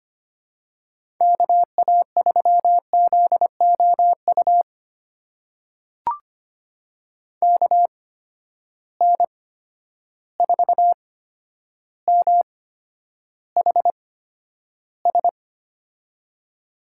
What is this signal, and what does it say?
Morse code, heard as a keyed tone, about 25 wpm, spaced as for 11 wpm, 700 Hz. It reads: KA3ZOU KN4MHS
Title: USA Call Signs - 6 Characters - 8x Character Spacing - 25wpm